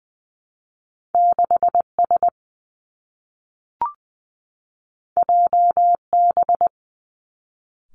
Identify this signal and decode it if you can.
Morse code, heard as a keyed tone, about 20 wpm, 700 Hz. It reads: 6 S J B
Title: Two Challenging Letter-Number Combinations 20wpm